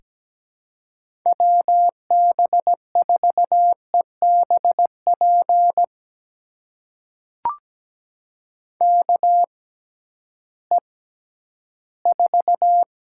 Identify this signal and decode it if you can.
Morse code, heard as a keyed tone, about 17 wpm, spaced as for 11 wpm, 700 Hz. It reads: WB4EBP KE4
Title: USA Call Signs - 6 Characters - 6x Character Spacing - 17wpm